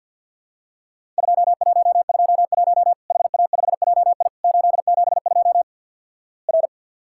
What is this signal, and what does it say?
Morse code, heard as a keyed tone, about 50 wpm, 700 Hz. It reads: 2121 HA5JI 872 R
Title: Worked All Europe DX Contest - QTC Exchange with Repeats - 50wpm